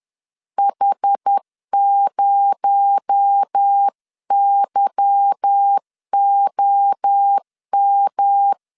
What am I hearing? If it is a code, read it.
Morse code, heard as a keyed tone, about 10 wpm, 800 Hz. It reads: H0YOM
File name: LEZIONI / 10 SIGNS / 10A SLASH 10wpm 2808